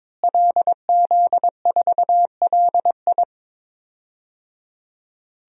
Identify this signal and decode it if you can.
Morse code, heard as a keyed tone, about 22 wpm, 700 Hz. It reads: LZ4LI
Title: DXCC-100 Call Signs - 5 Characters - Repeated 3x - 22wpm